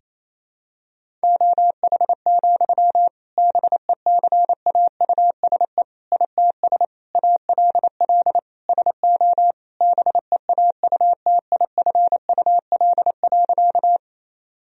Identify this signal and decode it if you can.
Morse code, heard as a keyed tone, about 28 wpm, 700 Hz. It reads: OH, BECAUSE ITS ALL SO BEAUTIFUL.
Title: Sentences from Top 1000 Words - Review Part 3 - 28wpm